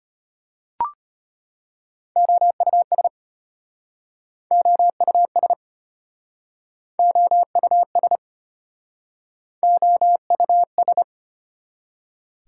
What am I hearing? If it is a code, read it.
Morse code, heard as a keyed tone, about 33 wpm, 700 Hz. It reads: OUS OUS OUS OUS